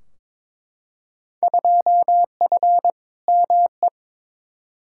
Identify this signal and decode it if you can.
Morse code, heard as a keyed tone, about 21 wpm, 700 Hz. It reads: 2F ME
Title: Field Day Exchanges - 22wpm